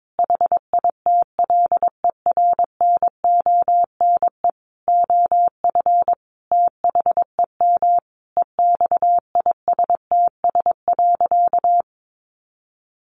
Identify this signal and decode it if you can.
Morse code, heard as a keyed tone, about 22 wpm, 700 Hz. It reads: HITLER NONE OF THEM EXISTS.
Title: A Sound Of Thunder 22wpm